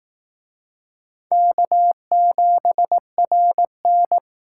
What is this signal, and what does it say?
Morse code, heard as a keyed tone, about 18 wpm, 700 Hz. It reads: K7RN
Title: Speed Racing - USA 1x2 Call Signs 15wpm